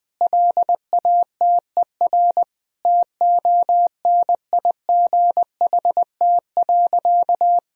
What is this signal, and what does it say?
Morse code, heard as a keyed tone, about 20 wpm, 700 Hz. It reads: LATER TONIGHT.
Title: Sentences from Top 1000 Words - Review Part 4 - 20wpm